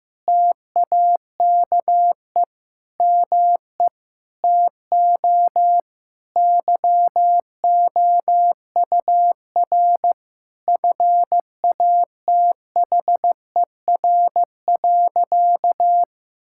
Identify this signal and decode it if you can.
Morse code, heard as a keyed tone, about 15 wpm, 700 Hz. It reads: TAKE ME TO YOUR FATHER.
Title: Sentences from Top 1000 Words - Review Part 6 - 15wpm